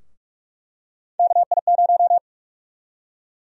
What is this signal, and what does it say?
Morse code, heard as a keyed tone, about 46 wpm, 700 Hz. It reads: KI0
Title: USA Call Sign Prefixes - Repeated 2x - 45wpm